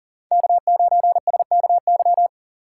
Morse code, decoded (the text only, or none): K9SKY